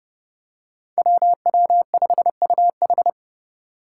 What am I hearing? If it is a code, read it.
Morse code, heard as a keyed tone, about 30 wpm, 700 Hz. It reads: WW5UH
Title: Five Challenging Letter-Number Combinations 30wpm